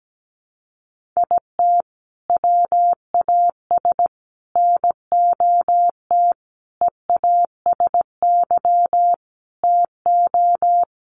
Morse code, heard as a keyed tone, about 17 wpm, 700 Hz. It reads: IT WAS NOT EASY TO
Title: Sentences from Top 1500 Words 17wpm